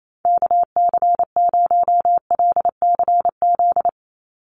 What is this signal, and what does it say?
Morse code, heard as a keyed tone, about 28 wpm, 700 Hz. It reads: KC0LCZ
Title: USA Call Signs 28wpm